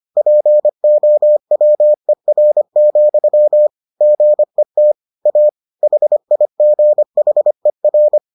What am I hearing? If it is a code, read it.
Morse code, heard as a keyed tone, about 25 wpm, 600 Hz. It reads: POWER, GET A HIGHER